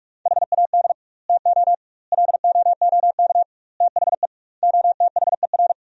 Morse code, heard as a keyed tone, about 46 wpm, 700 Hz. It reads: HAD TO LOOK THE OTHER
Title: 2x Word-Spacing - Sentences from Top 100 Words - 45wpm